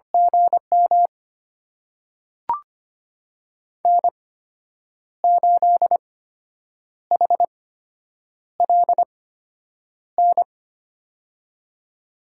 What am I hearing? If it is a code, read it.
Morse code, heard as a keyed tone, about 25 wpm, spaced as for 8 wpm, 700 Hz. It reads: GM N8HLN
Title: USA Call Signs - 5 Characters - 8x Character Spacing - 25wpm